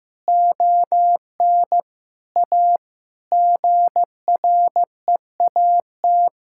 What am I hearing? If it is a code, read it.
Morse code, heard as a keyed tone, about 16 wpm, 700 Hz. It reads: ON A GREAT
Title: Sentences from Top 1000 Words - Review Part 5 - 15wpm